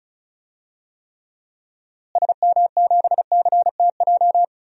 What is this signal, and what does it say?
Morse code, heard as a keyed tone, about 36 wpm, 700 Hz. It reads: SM7CTJ
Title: DXCC-100 Call Signs - 6 Characters - Repeated 2x - 35wpm